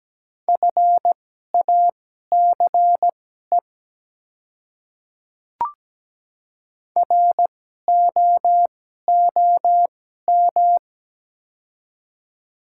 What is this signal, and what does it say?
Morse code, heard as a keyed tone, about 17 wpm, spaced as for 12 wpm, 700 Hz. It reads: FACE ROOM
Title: 2x Character Spacing - Top 300 Words - 17wpm